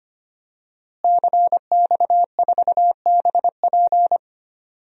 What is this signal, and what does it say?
Morse code, heard as a keyed tone, about 25 wpm, 700 Hz. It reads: CX4BP
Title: DXCC-100 Call Signs - 5 Characters - Repeated 3x - 25wpm